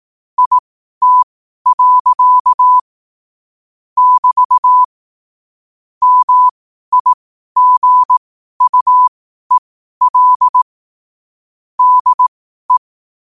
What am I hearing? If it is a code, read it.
Morse code, heard as a keyed tone, about 18 wpm, spaced as for 12 wpm, 1000 Hz. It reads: IT. <BT> MIGUEL DE
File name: qotdfiles / 12wpm